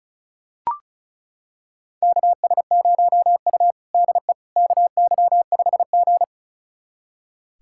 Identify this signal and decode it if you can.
Morse code, heard as a keyed tone, about 34 wpm, 700 Hz. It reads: KS0U DE KY5G
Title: USA Callsign DE Callsign - 4 Characters - 35wpm